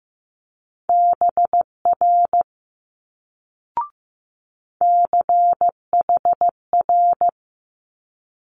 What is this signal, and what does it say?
Morse code, heard as a keyed tone, about 15 wpm, 700 Hz. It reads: BR CHR
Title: Multi-letter Phonemes Quick Fire 15wpm